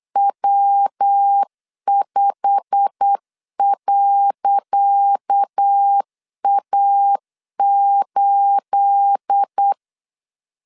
Morse code, heard as a keyed tone, about 8 wpm, 800 Hz. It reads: W5.A8
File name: LEZIONI / 10 SIGNS / 10C DOT 8wpm 3649